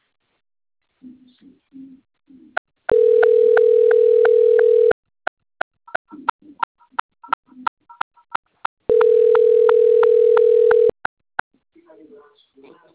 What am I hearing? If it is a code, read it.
Ringback tone (North America), 440 + 480 Hz, with speech.
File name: rg-1600-unknown- - - .49193